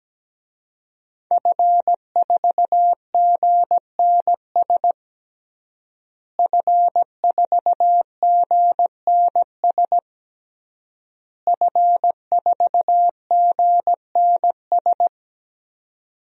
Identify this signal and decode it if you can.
Morse code, heard as a keyed tone, about 17 wpm, 700 Hz. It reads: F4GNS F4GNS F4GNS
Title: DXCC-100 Call Signs - 5 Characters - Repeated 3x - 17wpm